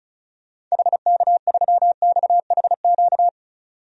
Morse code, heard as a keyed tone, about 36 wpm, 700 Hz. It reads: HK3XHQ
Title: DXCC-100 Call Signs - 6 Characters - 35wpm